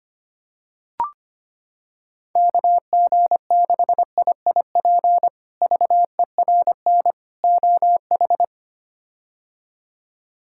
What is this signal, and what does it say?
Morse code, heard as a keyed tone, about 25 wpm, 700 Hz. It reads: KG6IIP VERN OH
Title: K1USN SST QSO Practice - 25wpm